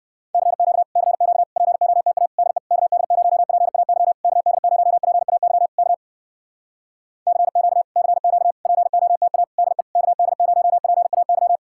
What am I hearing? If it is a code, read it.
Morse code, heard as a keyed tone, about 64 wpm, 700 Hz. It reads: CQ CQ POTA DE KD0PAY KD0PAY K CQ CQ POTA DE KD0PAY